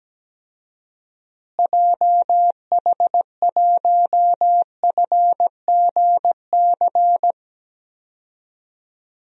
Morse code, heard as a keyed tone, about 17 wpm, 700 Hz. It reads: JH1FGC
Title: DXCC-100 Call Signs - 6 Characters - Repeated 3x - 17wpm